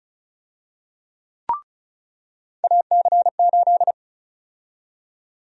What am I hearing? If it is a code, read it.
Morse code, heard as a keyed tone, about 34 wpm, 700 Hz. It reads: AC8
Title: USA Call Sign Prefixes - 35wpm